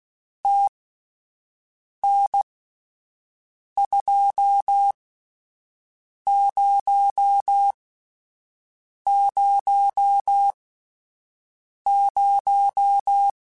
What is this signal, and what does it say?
Morse code, heard as a keyed tone, about 16 wpm, spaced as for 5 wpm, 800 Hz. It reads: TN2000
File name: QSOs / QSOs-mp3s / WD0FBT 1@10